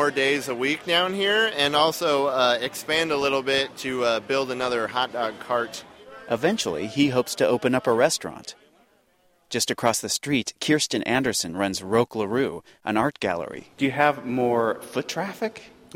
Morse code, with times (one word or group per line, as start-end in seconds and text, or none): none